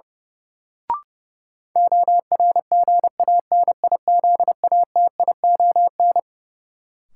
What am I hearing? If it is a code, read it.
Morse code, heard as a keyed tone, about 30 wpm, 700 Hz. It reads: ORGANIZATION
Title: Sentences from Top 1100 Words - New Word Drill Alt - 30wpm